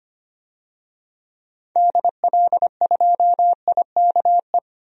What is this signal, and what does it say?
Morse code, heard as a keyed tone, about 25 wpm, 700 Hz. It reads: DL2IKE
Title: DXCC-100 Call Signs - 6 Characters - 25wpm